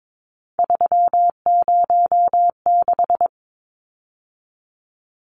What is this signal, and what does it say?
Morse code, heard as a keyed tone, about 22 wpm, 700 Hz. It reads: 306
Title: Numbers - 3-Digit 22wpm